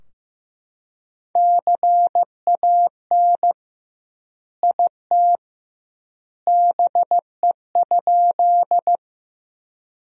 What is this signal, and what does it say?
Morse code, heard as a keyed tone, about 15 wpm, 700 Hz. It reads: CAN IT BE?
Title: 2x Word-Spacing - Sentences from Top 100 Words - 15wpm